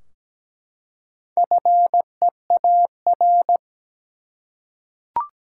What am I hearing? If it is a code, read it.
Morse code, heard as a keyed tone, about 17 wpm, 700 Hz. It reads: FEAR
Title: Four Letter Words - 17wpm